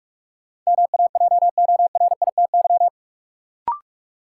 Morse code, heard as a keyed tone, about 46 wpm, 700 Hz. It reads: MAJORITY
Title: Sets of 3 Words - New Word Drill - 45wpm